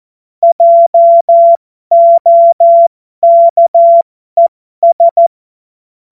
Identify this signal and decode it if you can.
Morse code, heard as a keyed tone, about 14 wpm, spaced as for 12 wpm, 680 Hz. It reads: JOKES
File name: Morsekurs / Witze